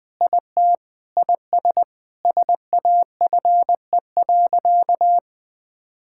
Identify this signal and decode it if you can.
Morse code, heard as a keyed tone, about 20 wpm, 700 Hz. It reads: IT IS SAFE.
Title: Sentences with 6-Word Phrases - 20wpm